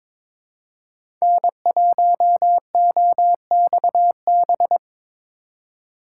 Morse code, heard as a keyed tone, about 22 wpm, 700 Hz. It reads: N1OXB
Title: USA Call Signs - 5 Characters - 4x Character Spacing - 22wpm